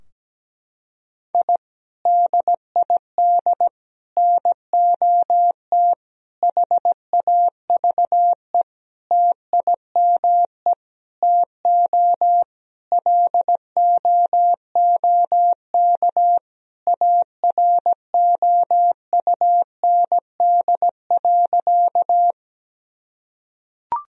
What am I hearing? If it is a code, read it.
Morse code, heard as a keyed tone, about 17 wpm, 700 Hz. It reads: I DID NOT HAVE TIME TO LOOK AROUND.
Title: Sentences from Top 1000 Words - Review Part 3 - 17wpm